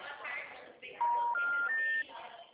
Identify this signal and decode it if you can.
Special information tone, with speech.
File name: out- -2006- - - .427